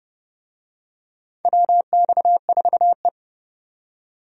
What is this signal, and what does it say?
Morse code, heard as a keyed tone, about 30 wpm, 700 Hz. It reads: WX4E
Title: USA Call Signs - 4 Characters - 30wpm